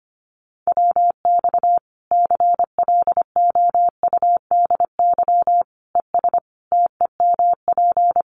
Morse code, heard as a keyed tone, about 25 wpm, 700 Hz. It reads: WX CLOUDY ES TEMP
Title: Common QSO Phrases 25wpm